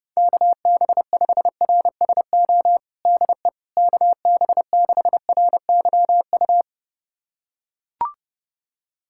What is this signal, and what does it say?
Morse code, heard as a keyed tone, about 32 wpm, 700 Hz. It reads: KB5RSO DE KB6RYU
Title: USA Callsign DE Callsign - 6 Characters - 30wpm